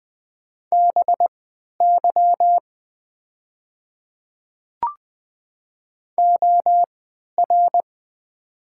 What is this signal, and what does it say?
Morse code, heard as a keyed tone, about 20 wpm, spaced as for 11 wpm, 700 Hz. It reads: BY OR
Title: 3x Character Spacing - Top 100 Words - 20wpm